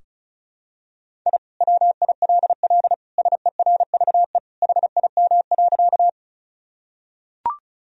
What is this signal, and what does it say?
Morse code, heard as a keyed tone, about 36 wpm, 700 Hz. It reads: I WILL SERVE HIM.
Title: Sentences from Top 1000 Words - Review Part 2 - 35wpm